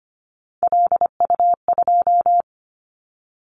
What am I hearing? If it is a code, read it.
Morse code, heard as a keyed tone, about 25 wpm, 700 Hz. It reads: LU2
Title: Callsign Prefixes for Top 100 DXCC - 25wpm